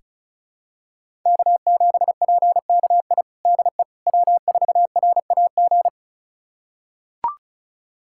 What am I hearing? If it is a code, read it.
Morse code, heard as a keyed tone, about 34 wpm, 700 Hz. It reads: K7PKI DE W4RAG
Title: USA Callsign DE Callsign - 5 Characters - 35wpm